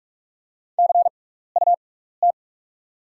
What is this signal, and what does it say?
Morse code, heard as a keyed tone, about 44 wpm, spaced as for 15 wpm, 700 Hz. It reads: CUT
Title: 6x Character Spacing - Top 500 Words - 45wpm